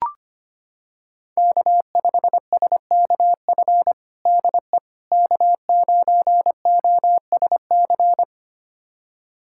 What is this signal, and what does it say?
Morse code, heard as a keyed tone, about 25 wpm, 700 Hz. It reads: K5SKF DE K9OSC
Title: USA Callsign DE Callsign - 5 Characters - 25wpm